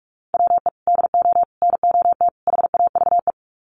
Morse code, heard as a keyed tone, about 44 wpm, 700 Hz. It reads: WE DO NOT HAVE
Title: Sets of 4 Words 45wpm